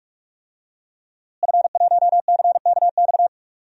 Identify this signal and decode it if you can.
Morse code, heard as a keyed tone, about 47 wpm, 700 Hz. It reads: F1CKX
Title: DXCC-100 Call Signs - 5 Characters - Repeated 2x - 45wpm